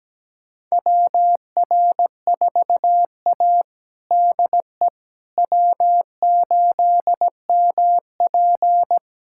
Morse code, heard as a keyed tone, about 17 wpm, 700 Hz. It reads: WR4A DE W8MP
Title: USA Callsign DE Callsign - 4 Characters - 17wpm